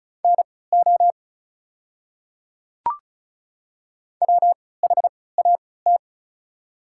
Morse code, heard as a keyed tone, about 36 wpm, spaced as for 20 wpm, 700 Hz. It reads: NO WHAT
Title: 3x Character Spacing - Top 100 Words - 35wpm